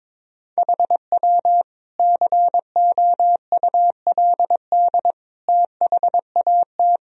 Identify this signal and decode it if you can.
Morse code, heard as a keyed tone, about 22 wpm, 700 Hz. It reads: HW COULD THAT
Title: Sentences from Top 400 Words and Abbreviations 22wpm